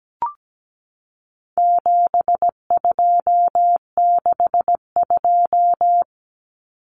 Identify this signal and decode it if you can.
Morse code, heard as a keyed tone, about 17 wpm, 700 Hz. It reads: 7262
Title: Common Cut Numbers - 4-Digit 17wpm